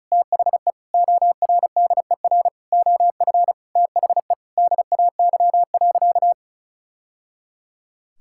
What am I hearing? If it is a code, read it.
Morse code, heard as a keyed tone, about 34 wpm, 700 Hz. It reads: THE ORDER OF THE DAY.
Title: Sentences from Top 1000 Words - Review Part 5 - 35wpm